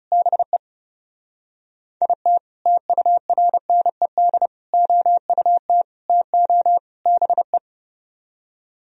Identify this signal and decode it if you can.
Morse code, heard as a keyed tone, about 35 wpm, 700 Hz. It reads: BE IT TURNED OUT TO BE
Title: Speed-Racing - Sets of 5 Words 30wpm